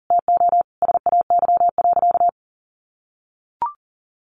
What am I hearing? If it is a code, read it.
Morse code, heard as a keyed tone, about 40 wpm, 700 Hz. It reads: TO SAY.
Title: Sentences from Top 1200 Words 40wpm